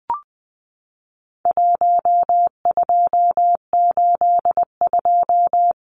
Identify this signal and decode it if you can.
Morse code, heard as a keyed tone, about 20 wpm, 700 Hz. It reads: 1282
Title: Numbers - 4-Digit 20wpm